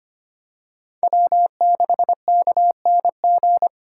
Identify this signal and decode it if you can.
Morse code, heard as a keyed tone, about 25 wpm, 700 Hz. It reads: W6KNG
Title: USA Call Signs - 5 Characters - 4x Character Spacing - 25wpm